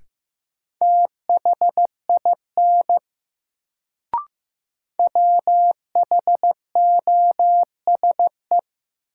Morse code, heard as a keyed tone, about 15 wpm, 700 Hz. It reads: THIN WHOSE
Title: Sentences from Top 1200 Words - New Word Drill Alt - 15wpm